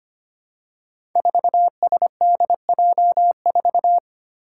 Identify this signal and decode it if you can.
Morse code, heard as a keyed tone, about 25 wpm, 700 Hz. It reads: 4SDJ4
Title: Five Challenging Letter-Number Combinations 25wpm